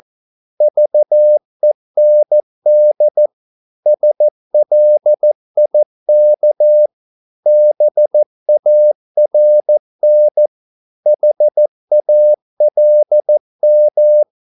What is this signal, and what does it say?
Morse code, heard as a keyed tone, about 14 wpm, 600 Hz. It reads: VEND SLIK BARN HALM
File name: Korte ord DK 14wpm